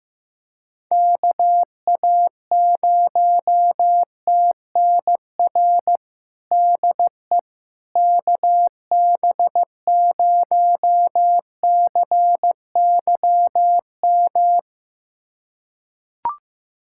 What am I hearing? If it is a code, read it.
Morse code, heard as a keyed tone, about 15 wpm, 700 Hz. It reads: KA0TNR DE KB0CYM
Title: USA Callsign DE Callsign 15wpm